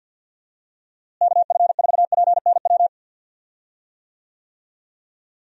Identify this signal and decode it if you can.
Morse code, heard as a keyed tone, about 51 wpm, 700 Hz. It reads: KF4PNW
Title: USA Call Signs - 6 Characters - Repeated 2x - 50wpm